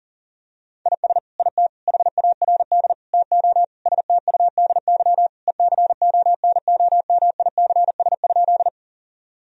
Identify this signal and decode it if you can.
Morse code, heard as a keyed tone, about 40 wpm, 700 Hz. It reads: IS IT HARD TO STUDY ECONOMICS?
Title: Sentences from Top 1000 Words - Review Part 3 - 40wpm